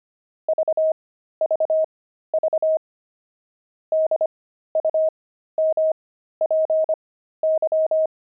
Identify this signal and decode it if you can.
Morse code, heard as a keyed tone, about 25 wpm, spaced as for 13 wpm, 630 Hz. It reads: VVV DUMPY
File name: pangram-fs-13-cs-25-kibitzer